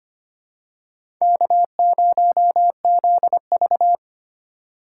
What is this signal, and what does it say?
Morse code, heard as a keyed tone, about 25 wpm, 700 Hz. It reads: K0ZV
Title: USA Call Signs - 4 Characters - 4x Character Spacing - 25wpm